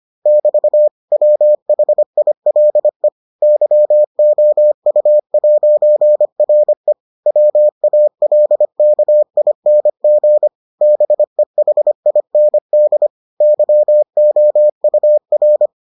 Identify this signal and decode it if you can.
Morse code, heard as a keyed tone, about 25 wpm, 600 Hz. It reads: <BT> WHILE YOU'RE WALKING BEHIND YOUR